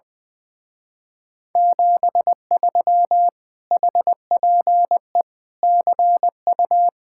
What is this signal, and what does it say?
Morse code, heard as a keyed tone, about 20 wpm, 700 Hz. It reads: 73 HPE CU
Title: Speed-Racing Common QSO Phrases Encore - 20wpm